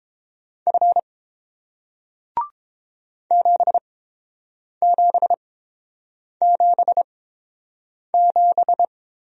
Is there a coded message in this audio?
Yes: Morse code from 0.7 to 8.8 s, heard as a keyed tone, about 30 wpm, spaced as for 7 wpm, 700 Hz. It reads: F 7777